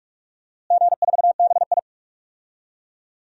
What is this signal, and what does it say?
Morse code, heard as a keyed tone, about 46 wpm, 700 Hz. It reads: G4BI
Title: DXCC-100 Call Signs - 4 Characters - Repeated 2x - 45wpm